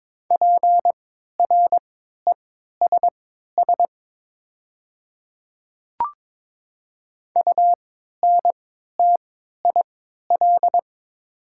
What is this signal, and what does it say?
Morse code, heard as a keyed tone, about 22 wpm, spaced as for 12 wpm, 700 Hz. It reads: PRESS UNTIL
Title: 3x Character Spacing - Top 300 Words - 22wpm